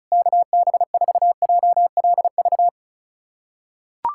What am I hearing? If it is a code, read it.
Morse code, heard as a keyed tone, about 34 wpm, 700 Hz. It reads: KB4JLV
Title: USA Callsign DE Callsign - 6 Characters - 35wpm